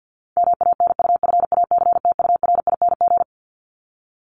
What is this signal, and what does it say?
Morse code, heard as a keyed tone, about 50 wpm, 700 Hz. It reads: MANUFACTURING
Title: ING Suffix - Top 100 Words with Base Words 50wpm